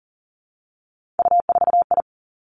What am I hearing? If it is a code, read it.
Morse code, heard as a keyed tone, about 41 wpm, 700 Hz. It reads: U4I
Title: Three Challenging Letter-Number Combinations 40wpm